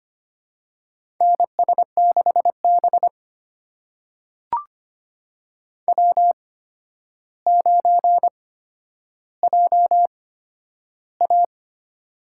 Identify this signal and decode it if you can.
Morse code, heard as a keyed tone, about 25 wpm, spaced as for 11 wpm, 700 Hz. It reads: NS6B W9JA
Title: USA Call Signs - 4 Characters - 8x Character Spacing - 25wpm